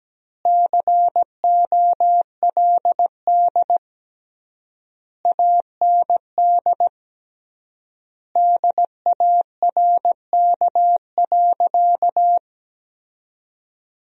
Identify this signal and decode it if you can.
Morse code, heard as a keyed tone, about 17 wpm, 700 Hz. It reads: COLD AND DARK.
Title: 3x Word-Spacing - Sentences from Top 500 Words - 17wpm